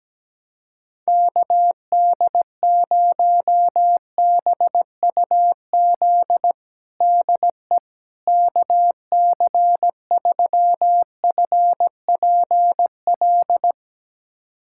Morse code, heard as a keyed tone, about 17 wpm, 700 Hz. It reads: KD0BUZ DE KC3FPL
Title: USA Callsign DE Callsign - 6 Characters - 17wpm